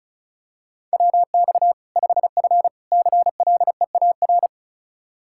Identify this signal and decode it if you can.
Morse code, heard as a keyed tone, about 36 wpm, 700 Hz. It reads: WX 5F CLEAR
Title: Practice QSO - Exchange 2 - Info - 35wpm